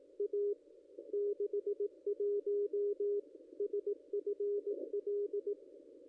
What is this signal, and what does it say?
Morse code, heard as a keyed tone, about 18 wpm, 410 Hz. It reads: A 61 SFL
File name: contests / ss-cw-k5tr-op-n5rz -0200